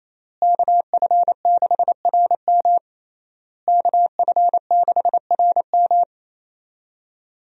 Morse code, heard as a keyed tone, about 28 wpm, 700 Hz. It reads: KF6RM KF6RM
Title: USA Call Signs - 5 Characters - Repeated 3x - 28wpm